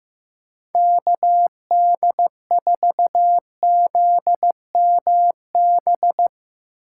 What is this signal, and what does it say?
Morse code, heard as a keyed tone, about 15 wpm, 700 Hz. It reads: KD4ZMB
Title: USA Call Signs - 6 Characters - 2x Character Spacing - 15wpm